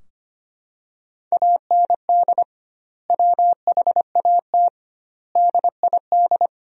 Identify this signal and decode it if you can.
Morse code, heard as a keyed tone, about 25 wpm, 700 Hz. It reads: AND WHAT DID
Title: 2x Word-Spacing - Sentences from Top 300 Words - 25wpm